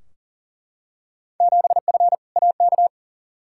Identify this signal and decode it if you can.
Morse code, heard as a keyed tone, about 40 wpm, 700 Hz. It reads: 7F AK
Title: Field Day Exchanges Encore - 40wpm